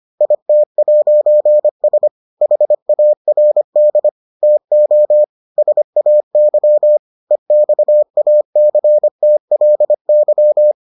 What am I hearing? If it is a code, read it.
Morse code, heard as a keyed tone, about 25 wpm, 600 Hz. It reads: IT'S HARD TO SAY EXACTLY